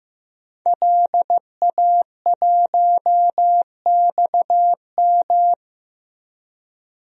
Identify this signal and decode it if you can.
Morse code, heard as a keyed tone, about 15 wpm, 700 Hz. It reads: LA1XM
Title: DXCC-100 Call Signs - 5 Characters - 15wpm